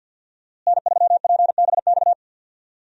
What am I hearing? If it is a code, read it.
Morse code, heard as a keyed tone, about 49 wpm, 700 Hz. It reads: N3PBX
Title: USA Call Signs - 5 Characters - Repeated 2x - 50wpm